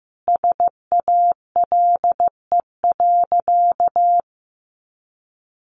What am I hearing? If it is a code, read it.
Morse code, heard as a keyed tone, about 15 wpm, 700 Hz. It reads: SALE.
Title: Sentences from Top 1400 Words 15wpm